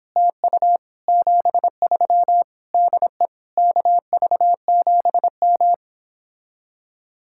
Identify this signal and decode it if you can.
Morse code, heard as a keyed tone, about 31 wpm, 700 Hz. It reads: TU 73 DE KV7M